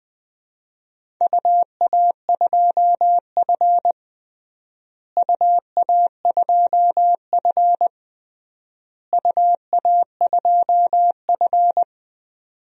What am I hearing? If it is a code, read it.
Morse code, heard as a keyed tone, about 19 wpm, 700 Hz. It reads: UA2F UA2F UA2F
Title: DXCC-100 Call Signs - 4 Characters - Repeated 3x - 20wpm